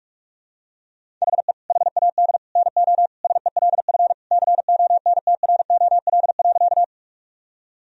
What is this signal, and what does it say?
Morse code, heard as a keyed tone, about 46 wpm, 700 Hz. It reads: HE HAD NO SELF CONTROL.
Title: Sentences from Top 1000 Words - Review Part 3 - 45wpm